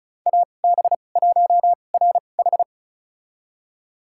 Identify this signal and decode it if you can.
Morse code, heard as a keyed tone, about 36 wpm, spaced as for 25 wpm, 700 Hz. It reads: AB1RH
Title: USA Call Signs - 5 Characters - 2x Character Spacing - 35wpm